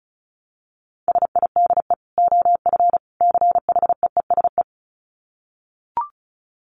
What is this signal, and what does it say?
Morse code, heard as a keyed tone, about 34 wpm, 700 Hz. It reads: SIDE OF CHEESE
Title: Dit Challenge 35wpm